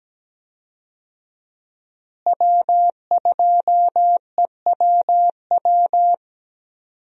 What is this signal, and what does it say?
Morse code, heard as a keyed tone, about 17 wpm, 700 Hz. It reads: W2EWW
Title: USA Call Signs - 5 Characters - Repeated 2x - 17wpm